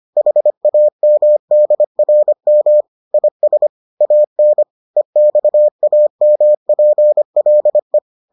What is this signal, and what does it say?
Morse code, heard as a keyed tone, about 25 wpm, 600 Hz. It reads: HAMDRM IS AN EXAMPLE